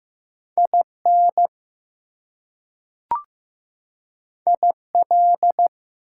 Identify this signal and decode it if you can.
Morse code, heard as a keyed tone, about 15 wpm, 700 Hz. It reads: IN IL
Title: US States and Canadian Provinces 15wpm